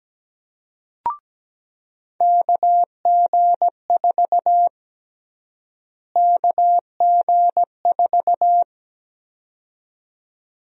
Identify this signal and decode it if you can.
Morse code, heard as a keyed tone, about 17 wpm, 700 Hz. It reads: KG4 KG4
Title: USA Call Sign Prefixes - Repeated 2x - 17wpm